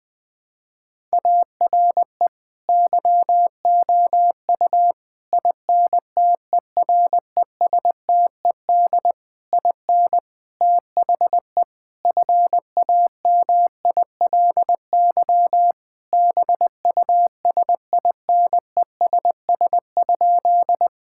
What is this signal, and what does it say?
Morse code, heard as a keyed tone, about 20 wpm, 700 Hz. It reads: ARE YOU INTERESTED IN THE FAMILY BUSINESS?
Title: Sentences from Top 1000 Words - Review Part 6 - 20wpm